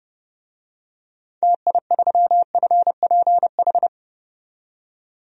Morse code, heard as a keyed tone, about 31 wpm, 700 Hz. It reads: TI3FPH
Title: DXCC-100 Call Signs - 6 Characters - 30wpm